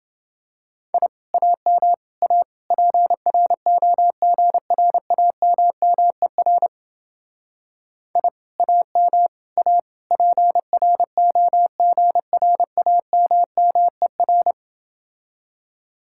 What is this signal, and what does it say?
Morse code, heard as a keyed tone, about 28 wpm, 700 Hz. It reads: I AM A PROGRAMMER I AM A PROGRAMMER